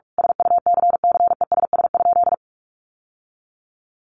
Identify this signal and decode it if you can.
Morse code, heard as a keyed tone, about 44 wpm, 700 Hz. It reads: SUCCESS?
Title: Sentences from Top 900 Words 45wpm